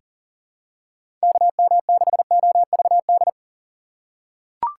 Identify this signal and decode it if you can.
Morse code, heard as a keyed tone, about 39 wpm, 700 Hz. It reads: KM6OVD
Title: USA Call Signs - 6 Characters - 3x Character Spacing - 40wpm